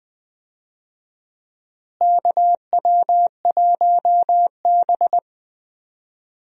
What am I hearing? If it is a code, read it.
Morse code, heard as a keyed tone, about 20 wpm, 700 Hz. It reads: KW1B
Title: USA Call Signs - 4 Characters - Repeated 3x - 20wpm